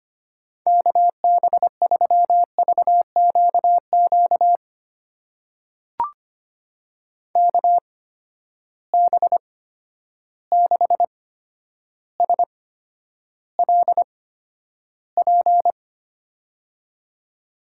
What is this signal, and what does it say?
Morse code, heard as a keyed tone, about 25 wpm, spaced as for 11 wpm, 700 Hz. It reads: KB3VQQ KB6SLP
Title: USA Call Signs - 6 Characters - 8x Character Spacing - 25wpm